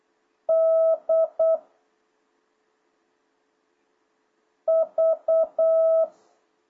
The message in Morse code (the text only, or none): D V